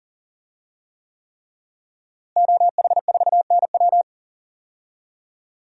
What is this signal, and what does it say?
Morse code, heard as a keyed tone, about 40 wpm, 700 Hz. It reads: OH4NW
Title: DXCC-100 Call Signs - 5 Characters - Repeated 3x - 40wpm